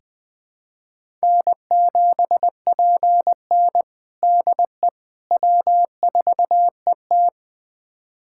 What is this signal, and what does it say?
Morse code, heard as a keyed tone, about 20 wpm, 700 Hz. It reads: N7PN DE W4ET
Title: USA Callsign DE Callsign - 4 Characters - 20wpm